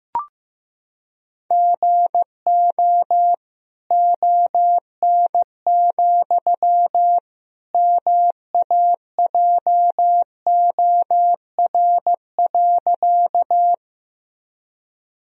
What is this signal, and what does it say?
Morse code, heard as a keyed tone, about 16 wpm, 700 Hz. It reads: GO ON, MAJOR.
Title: Sentences from Top 1000 Words - Review Part 4 - 15wpm